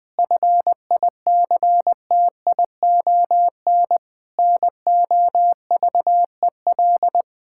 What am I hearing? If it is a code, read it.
Morse code, heard as a keyed tone, about 20 wpm, 700 Hz. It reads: FICTION NOVEL
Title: Difficult Words with Hints - 20wpm